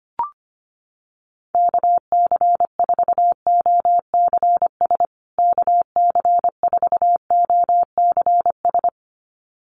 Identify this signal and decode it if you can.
Morse code, heard as a keyed tone, about 25 wpm, 700 Hz. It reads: KC4OCS KC4OCS
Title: USA Call Signs Repeated 2x 25wpm